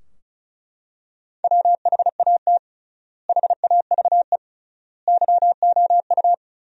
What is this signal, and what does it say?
Morse code, heard as a keyed tone, about 36 wpm, 700 Hz. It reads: WHAT HAVE YOU
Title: 3x Word-Spacing - Sentences from Top 500 Words - 35wpm